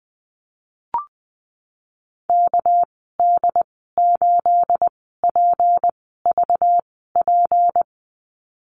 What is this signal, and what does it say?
Morse code, heard as a keyed tone, about 20 wpm, spaced as for 15 wpm, 700 Hz. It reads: KD8PVP
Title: 219 Lesson 35g - Number 0 - Call Signs 20wpm